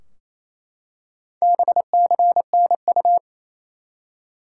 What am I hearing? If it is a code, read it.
Morse code, heard as a keyed tone, about 28 wpm, 700 Hz. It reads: BCNU
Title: Top 90 QSO Elements 28wpm